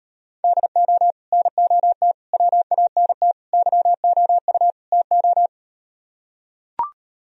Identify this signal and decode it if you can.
Morse code, heard as a keyed tone, about 38 wpm, 700 Hz. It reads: DO NOT WANT YOU TO